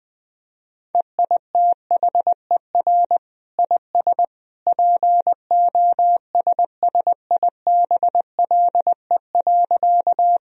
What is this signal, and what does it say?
Morse code, heard as a keyed tone, about 20 wpm, 700 Hz. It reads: EITHER IS POSSIBLE.
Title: Sentences from Top 1000 Words - Review Part 4 - 20wpm